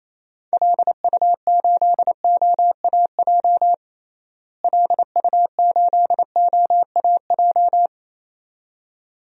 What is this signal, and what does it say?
Morse code, heard as a keyed tone, about 28 wpm, 700 Hz. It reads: LU8OAJ LU8OAJ
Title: DXCC-100 Call Signs - 6 Characters - Repeated 2x - 28wpm